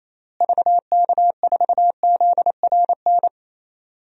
Morse code, heard as a keyed tone, about 28 wpm, 700 Hz. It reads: VK4ZRN
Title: DXCC-100 Call Signs - 6 Characters - Repeated 2x - 28wpm